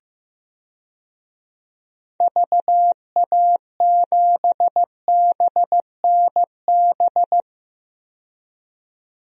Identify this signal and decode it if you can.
Morse code, heard as a keyed tone, about 15 wpm, 700 Hz. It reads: VA7BNB
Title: DXCC-100 Call Signs - 6 Characters - 15wpm